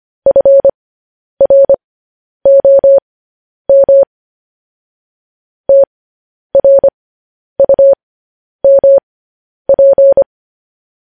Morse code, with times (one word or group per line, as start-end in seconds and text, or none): 0.3-4.0 FROM
5.7-10.2 TRUMP